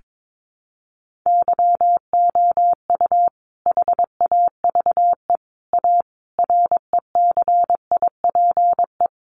Morse code, heard as a keyed tone, about 22 wpm, 700 Hz. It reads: YOU HAVE A RECIPE
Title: Sentences from Top 1200 Words 22wpm